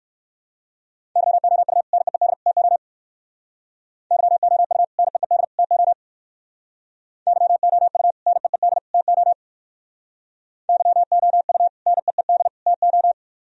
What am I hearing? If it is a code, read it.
Morse code, heard as a keyed tone, about 55 wpm, 700 Hz. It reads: YOU NEED TO YOU NEED TO YOU NEED TO YOU NEED TO
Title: Speed-Racing - Sets of 3 Words 45wpm